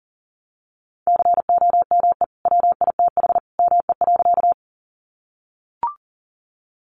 Morse code, heard as a keyed tone, about 40 wpm, 700 Hz. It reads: COME WITH ME.
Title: Sentences from Top 200 Words 40wpm